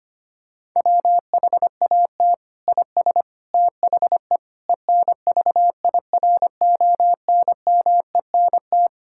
Morse code, heard as a keyed tone, about 25 wpm, 700 Hz. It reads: WHAT IS THE ENVIRONMENT
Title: Sentences from Top 1000 Words - Review Part 5 - 25wpm